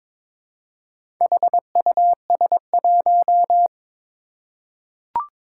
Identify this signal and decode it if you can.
Morse code, heard as a keyed tone, about 23 wpm, 700 Hz. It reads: HUS1
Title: Four Challenging Letter-Number Combinations 22wpm